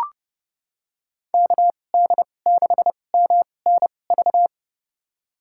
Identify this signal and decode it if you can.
Morse code, heard as a keyed tone, about 30 wpm, spaced as for 22 wpm, 700 Hz. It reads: KD6MNV
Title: USA Call Signs - 6 Characters - 2x Character Spacing - 30wpm